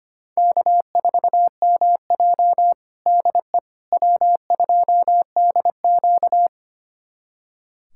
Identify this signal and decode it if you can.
Morse code, heard as a keyed tone, about 25 wpm, 700 Hz. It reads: K4MJ DE W2DQ
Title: USA Callsign DE Callsign - 4 Characters - 25wpm